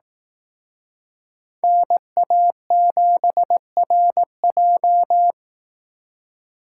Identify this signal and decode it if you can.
Morse code, heard as a keyed tone, about 18 wpm, 700 Hz. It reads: NA7RJ
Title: Speed Racing - USA 2x2 Call Signs 15wpm